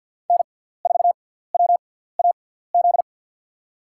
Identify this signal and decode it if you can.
Morse code, heard as a keyed tone, about 52 wpm, spaced as for 17 wpm, 700 Hz. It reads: N4WAZ
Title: USA Call Signs - 6x Character Spacing - 50wpm